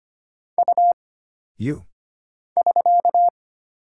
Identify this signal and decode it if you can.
Morse code, heard as a keyed tone, about 25 wpm, 700 Hz. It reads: U <SK>
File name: Single Character (Letter:Number:Punctuation:Prosign) - Mind-Melt - 25wpm